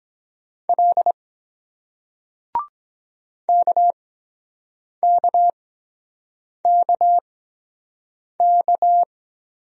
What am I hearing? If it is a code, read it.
Morse code, heard as a keyed tone, about 23 wpm, spaced as for 7 wpm, 700 Hz. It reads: L KKKK